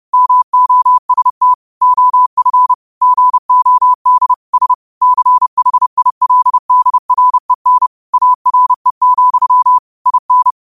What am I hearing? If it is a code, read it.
Morse code, heard as a keyed tone, about 31 wpm, 1000 Hz. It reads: MOST OF GOD S CHILDREN ARE, IN
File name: qotdfiles / 30wpm